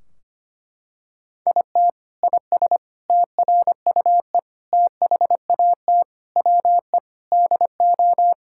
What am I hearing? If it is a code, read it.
Morse code, heard as a keyed tone, about 25 wpm, 700 Hz. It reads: IT IS TRUE THAT WE DO
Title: Sentences from Top 1600 Words 25wpm